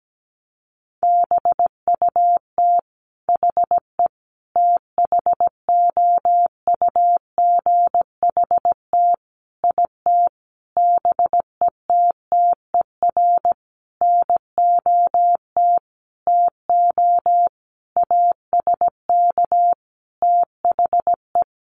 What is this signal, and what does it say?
Morse code, heard as a keyed tone, about 17 wpm, 700 Hz. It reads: BUT HE THOUGHT IT BETTER NOT TO ASK THE
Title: Sentences from Top 600 Words 17wpm